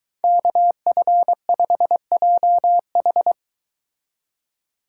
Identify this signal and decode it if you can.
Morse code, heard as a keyed tone, about 23 wpm, 700 Hz. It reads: KF5JH